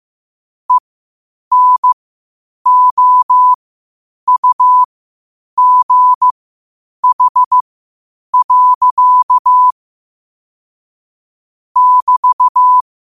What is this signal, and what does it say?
Morse code, heard as a keyed tone, about 15 wpm, spaced as for 8 wpm, 1000 Hz. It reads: ENOUGH. <BT>
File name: qotdfiles / 8wpm